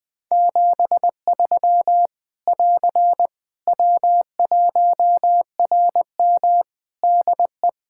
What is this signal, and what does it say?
Morse code, heard as a keyed tone, about 21 wpm, 700 Hz. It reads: 73 <AR> W1RM DE
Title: Practice QSO - Exchange 4 - Ending - 20wpm